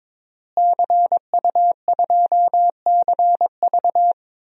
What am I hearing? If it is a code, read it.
Morse code, heard as a keyed tone, about 22 wpm, 700 Hz. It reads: CU2CV
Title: DXCC-100 Call Signs - 5 Characters - Repeated 3x - 22wpm